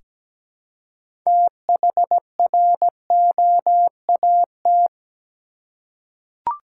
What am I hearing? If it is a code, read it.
Morse code, heard as a keyed tone, about 17 wpm, 700 Hz. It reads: THROAT
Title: Six Letter Words - 17wpm